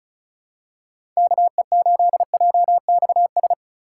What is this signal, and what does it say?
Morse code, heard as a keyed tone, about 36 wpm, 700 Hz. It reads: KE8JXS
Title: USA Call Signs - 8x Character Spacing - 35wpm